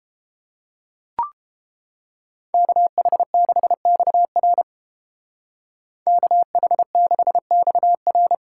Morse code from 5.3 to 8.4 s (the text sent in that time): KH6XR